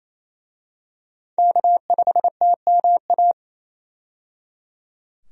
Morse code, heard as a keyed tone, about 27 wpm, 700 Hz. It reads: K5TMA
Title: USA Call Signs - 5 Characters - 28wpm